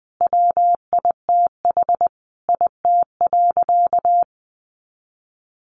Morse code, heard as a keyed tone, about 19 wpm, 700 Hz. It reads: WITH IT.
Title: Additional Easy Phrases 20wpm